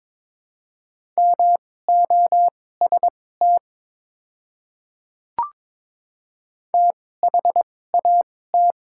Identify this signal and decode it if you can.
Morse code, heard as a keyed tone, about 22 wpm, spaced as for 16 wpm, 700 Hz. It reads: MOST THAT
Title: 2x Character Spacing - Top 100 Words - 22wpm